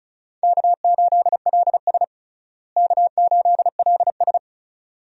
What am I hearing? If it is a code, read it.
Morse code, heard as a keyed tone, about 36 wpm, 700 Hz. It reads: K8LS K8LS
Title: USA Call Signs - 4 Characters - Repeated 2x - 35wpm